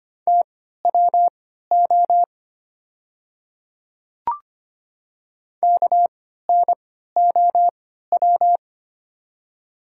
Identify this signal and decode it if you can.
Morse code, heard as a keyed tone, about 25 wpm, spaced as for 14 wpm, 700 Hz. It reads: TWO KNOW
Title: 3x Character Spacing - Top 300 Words - 25wpm